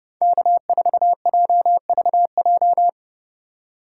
Morse code, heard as a keyed tone, about 29 wpm, 700 Hz. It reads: K4JVJ
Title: USA Call Signs - 5 Characters - 3x Character Spacing - 30wpm